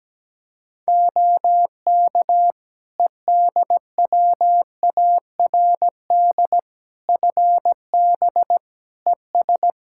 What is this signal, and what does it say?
Morse code, heard as a keyed tone, about 17 wpm, 700 Hz. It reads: OK EDWARD FB ES
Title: Practice QSO - Exchange 2 - Info - 17wpm